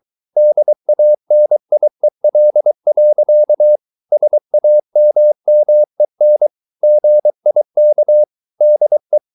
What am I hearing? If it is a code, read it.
Morse code, heard as a keyed tone, about 23 wpm, 600 Hz. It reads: DANIEL. SAMMEN GIK DE
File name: Ferie QRS 23wpm